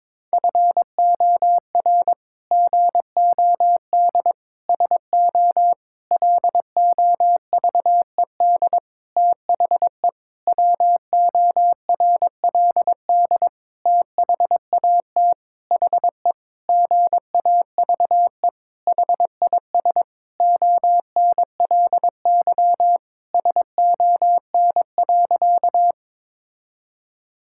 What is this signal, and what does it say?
Morse code, heard as a keyed tone, about 22 wpm, 700 Hz. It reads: FOR GOD SO LOVED THE WORLD THAT HE GAVE HIS ONLY SON.
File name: practice / 22wpm / Sentences with 8-Word Phrases - 22wpm